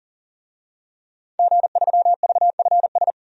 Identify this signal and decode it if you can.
Morse code, heard as a keyed tone, about 37 wpm, 700 Hz. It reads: G3VFS
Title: DXCC-100 Call Signs - 5 Characters - 40wpm